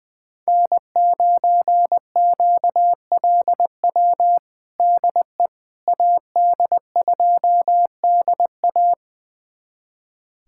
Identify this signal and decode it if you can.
Morse code, heard as a keyed tone, about 20 wpm, 700 Hz. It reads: N9QLW DE AD2DA
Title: USA Callsign DE Callsign - 5 Characters - 20wpm